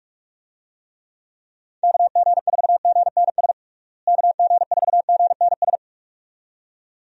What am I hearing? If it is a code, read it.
Morse code, heard as a keyed tone, about 41 wpm, 700 Hz. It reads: KG4GNS KG4GNS
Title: USA Call Signs - 6 Characters - Repeated 2x - 45wpm